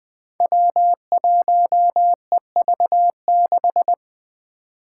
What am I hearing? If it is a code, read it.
Morse code, heard as a keyed tone, about 20 wpm, 700 Hz. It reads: W1EV6
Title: Five Challenging Letter-Number Combinations 20wpm